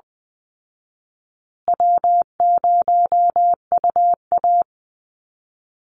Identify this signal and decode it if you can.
Morse code, heard as a keyed tone, about 20 wpm, 700 Hz. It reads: W0UA
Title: Easy Call Signs Repeated 3x - 20wpm